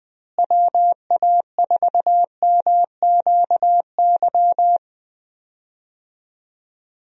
Morse code, heard as a keyed tone, about 20 wpm, 700 Hz. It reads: WA4MQY
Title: USA Call Signs - 6 Characters - Repeated 3x - 20wpm